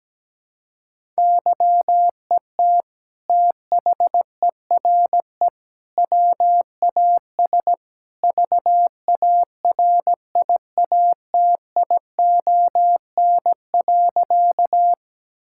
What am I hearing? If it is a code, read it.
Morse code, heard as a keyed tone, about 17 wpm, 700 Hz. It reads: YET THERE WAS VARIATION.
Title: Sentences from Top 1600 Words 17wpm